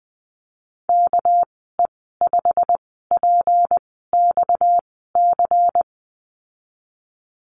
Morse code, heard as a keyed tone, about 20 wpm, spaced as for 15 wpm, 700 Hz. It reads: KE5PXC
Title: 245 Lesson 39g - Letter z - Call Signs 20wpm